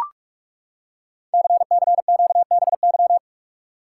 Morse code, heard as a keyed tone, about 44 wpm, 690 Hz. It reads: CCQBY
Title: Code Groups - 5 Letters - 45wpm